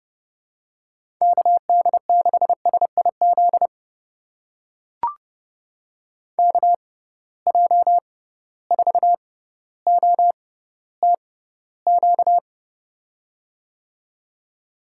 Morse code, heard as a keyed tone, about 30 wpm, spaced as for 15 wpm, 700 Hz. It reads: KD6SIZ KJ4OTQ
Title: USA Call Signs - 6 Characters - 6x Character Spacing - 30wpm